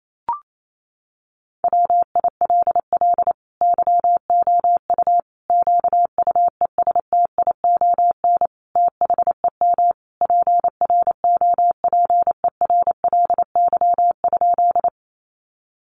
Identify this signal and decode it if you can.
Morse code, heard as a keyed tone, about 28 wpm, 700 Hz. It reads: WILL YOU QUESTION THEM PROPERLY?
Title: Sentences from Top 1100 Words 28wpm